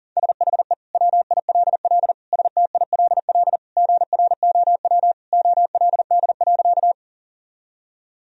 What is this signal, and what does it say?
Morse code, heard as a keyed tone, about 40 wpm, 700 Hz. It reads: SHE WILL STILL GROW OLD.
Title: Sentences from Top 1000 Words - Review Part 1 - 40wpm